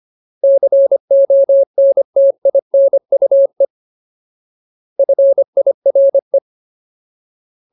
Morse code, heard as a keyed tone, about 25 wpm, 550 Hz. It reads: CONTINUE FIRE
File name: Ultimate Word List / 25WPM Ultimate